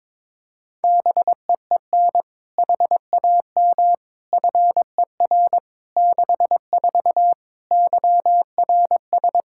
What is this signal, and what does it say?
Morse code, heard as a keyed tone, about 22 wpm, 700 Hz. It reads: BEEN HAM FER 64 YRS
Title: K1USN SST QSO Practice 22wpm